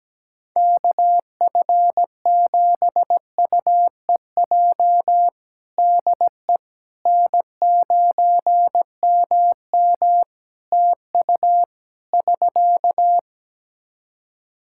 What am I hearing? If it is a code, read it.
Morse code, heard as a keyed tone, about 17 wpm, 700 Hz. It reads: KF7UEJ DE N9MM TU <SK>
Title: Practice QSO - Exchange 4 - Ending - 17wpm